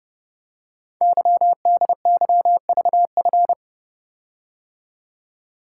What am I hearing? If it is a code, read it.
Morse code, heard as a keyed tone, about 30 wpm, 700 Hz. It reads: YDYVF
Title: Code Groups - 5 Letters - 30wpm